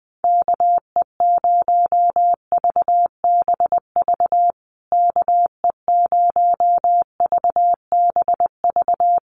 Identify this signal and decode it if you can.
Morse code, heard as a keyed tone, about 20 wpm, 700 Hz. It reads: KE0VBV KE0VBV
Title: USA Call Signs Repeated 3x 20wpm